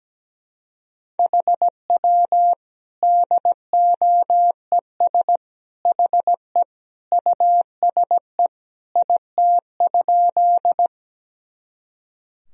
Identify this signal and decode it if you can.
Morse code, heard as a keyed tone, about 17 wpm, 700 Hz. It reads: HW DOES HE USE IT?
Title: Sentences from Top 200 Words and Abbreviations 17wpm